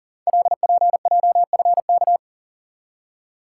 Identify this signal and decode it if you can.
Morse code, heard as a keyed tone, about 42 wpm, 690 Hz. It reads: LPJFK
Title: Code Groups - 5 Letters - 40wpm